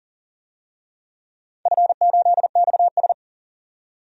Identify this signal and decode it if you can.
Morse code, heard as a keyed tone, about 38 wpm, 700 Hz. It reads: F8XS
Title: DXCC-100 Call Signs - 4 Characters - 40wpm